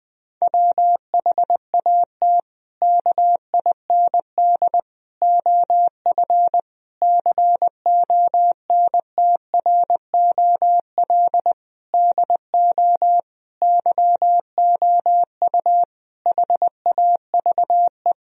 Morse code, heard as a keyed tone, about 20 wpm, 700 Hz. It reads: WHAT KIND OF CONTROL DO YOU HAVE
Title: Sentences from Top 1000 Words - Review Part 3 - 20wpm